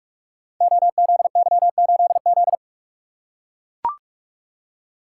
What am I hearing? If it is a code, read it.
Morse code, heard as a keyed tone, about 44 wpm, 700 Hz. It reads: OZY8Z
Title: Code Groups - 5 Letter:Numbers - 45wpm